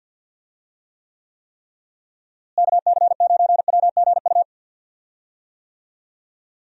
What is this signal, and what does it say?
Morse code, heard as a keyed tone, about 50 wpm, 700 Hz. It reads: KC9WGU
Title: USA Call Signs - 6 Characters - Repeated 3x - 50wpm